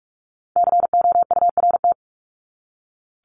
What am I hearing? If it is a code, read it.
Morse code, heard as a keyed tone, about 44 wpm, 700 Hz. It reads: COURT
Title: Top 1501-1750 Words 45wpm